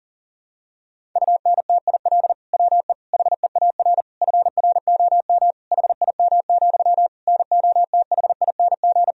Morse code, heard as a keyed tone, about 40 wpm, 700 Hz. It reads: UNTIL WE HEAR FROM HIM, NOTHING
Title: Sentences from Top 1000 Words - Review Part 1 - 40wpm